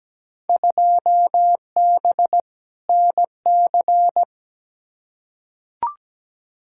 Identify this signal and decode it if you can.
Morse code, heard as a keyed tone, about 17 wpm, 700 Hz. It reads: 2B NC
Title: Field Day Exchanges Encore - 17wpm